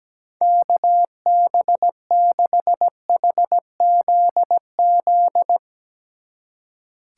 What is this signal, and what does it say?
Morse code, heard as a keyed tone, about 17 wpm, 700 Hz. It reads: KB6HZZ
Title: Speed Racing - USA 2x3 Call Signs 17wpm